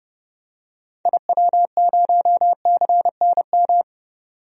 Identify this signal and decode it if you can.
Morse code, heard as a keyed tone, about 30 wpm, 700 Hz. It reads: IW0CNM
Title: DXCC-100 Call Signs - 6 Characters - 30wpm